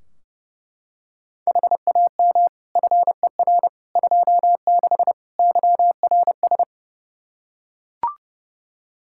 Morse code, heard as a keyed tone, about 30 wpm, 700 Hz. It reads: HAM FER 26 YRS
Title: Practice QSO - Exchange 3 - More Info - 30wpm